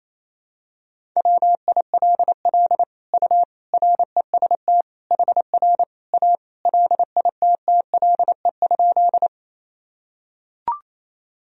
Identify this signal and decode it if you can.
Morse code, heard as a keyed tone, about 28 wpm, 700 Hz. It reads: WILL U REST HR A LITTLE?
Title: Sentences from Top 500 Words and Abbreviations 28wpm